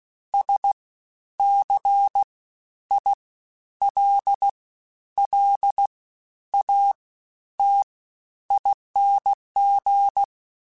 Morse code, heard as a keyed tone, about 16 wpm, 800 Hz. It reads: S C I L L A T ING
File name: Special INGs-mp3s / Special INGs-mp3s / SlowToFaster ing @15